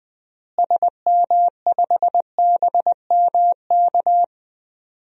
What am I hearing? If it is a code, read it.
Morse code, heard as a keyed tone, about 20 wpm, 700 Hz. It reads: SM5BMK
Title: DXCC-100 Call Signs - 6 Characters - Repeated 2x - 20wpm